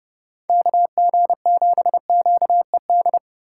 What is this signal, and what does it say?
Morse code, heard as a keyed tone, about 36 wpm, 700 Hz. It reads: KG7QED
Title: Speed Racing - USA 2x3 Call Signs 30wpm